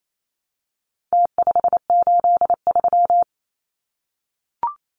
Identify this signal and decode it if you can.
Morse code, heard as a keyed tone, about 28 wpm, 700 Hz. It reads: T583
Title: Numbers - Contest Style - 4-Digit 28wpm